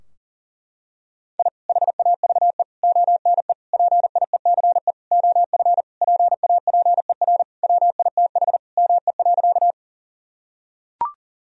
Morse code, heard as a keyed tone, about 40 wpm, 700 Hz. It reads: I HAVE ONE PIECE OF PAPER WITH ME.
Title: Sentences from Top 1000 Words - Review Part 2 - 40wpm